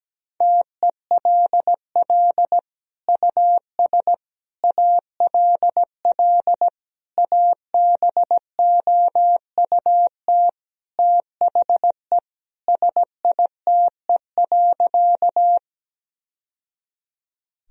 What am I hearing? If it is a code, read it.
Morse code, heard as a keyed tone, about 17 wpm, 700 Hz. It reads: TELL US ALL ABOUT THE SITE.
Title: Sentences from Top 1000 Words - Review Part 4 - 17wpm